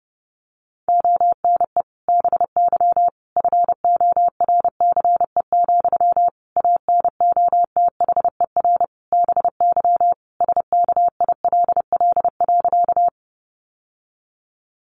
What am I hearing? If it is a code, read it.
Morse code, heard as a keyed tone, about 30 wpm, 700 Hz. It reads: ONE BY FORCE, ANOTHER BY SKILL.
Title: Sentences from Top 1200 Words 30wpm